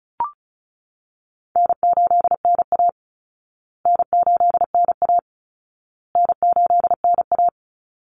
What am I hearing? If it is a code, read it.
Morse code, heard as a keyed tone, about 34 wpm, 700 Hz. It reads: N8NA N8NA N8NA
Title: Easy Call Signs Repeated 3x - 35wpm